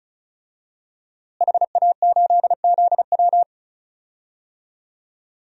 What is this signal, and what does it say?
Morse code, heard as a keyed tone, about 36 wpm, 700 Hz. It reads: HA8ZW
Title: DXCC-100 Call Signs - 5 Characters - Repeated 2x - 35wpm